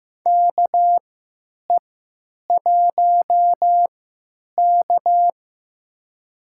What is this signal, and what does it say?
Morse code, heard as a keyed tone, about 15 wpm, spaced as for 9 wpm, 700 Hz. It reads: KE1K
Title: USA Call Signs - 4 Characters - 3x Character Spacing - 15wpm